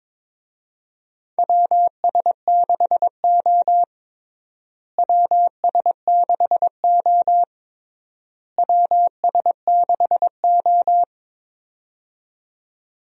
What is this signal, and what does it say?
Morse code, heard as a keyed tone, about 21 wpm, 700 Hz. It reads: WS6O WS6O WS6O
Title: USA Call Signs - 4 Characters - Repeated 3x - 22wpm